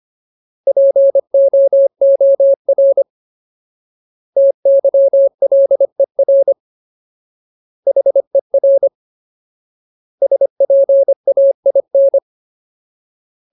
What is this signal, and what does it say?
Morse code, heard as a keyed tone, about 25 wpm, 550 Hz. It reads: POOR TYLER HER SPAIN
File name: Ultimate Word List / 25WPM Ultimate